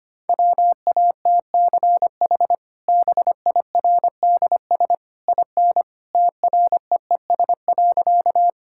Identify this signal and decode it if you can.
Morse code, heard as a keyed tone, about 25 wpm, 700 Hz. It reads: WATCH BIRDS IN TREES.
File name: Easy Sentences with Hints - 25wpm